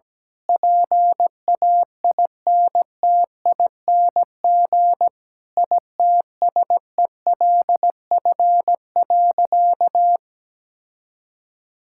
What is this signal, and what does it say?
Morse code, heard as a keyed tone, about 17 wpm, 700 Hz. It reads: PAINTING ITSELF.
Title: Sentences from Top 1000 Words - Review Part 6 - 17wpm